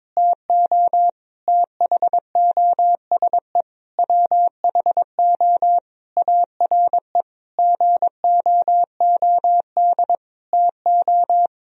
Morse code, heard as a keyed tone, about 22 wpm, 700 Hz. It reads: TO THOSE WHO ARE GOOD TO
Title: Sentences from Top 1000 Words - Review Part 1 - 22wpm